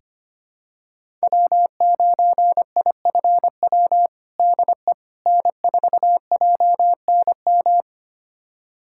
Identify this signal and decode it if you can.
Morse code, heard as a keyed tone, about 25 wpm, 700 Hz. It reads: W9IFW DE N4JNM
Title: USA Callsign DE Callsign - 5 Characters - 25wpm